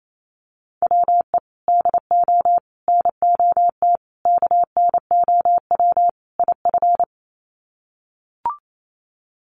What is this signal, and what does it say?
Morse code, heard as a keyed tone, about 28 wpm, 700 Hz. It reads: WE DO NOT KNOW IF
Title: Sets of 5 Words Encore2 28wpm